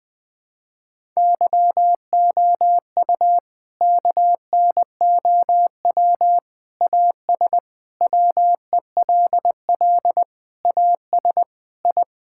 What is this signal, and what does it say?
Morse code, heard as a keyed tone, about 20 wpm, 700 Hz. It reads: YOU KNOW AS WELL AS I
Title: Sets of 8 Words - 20wpm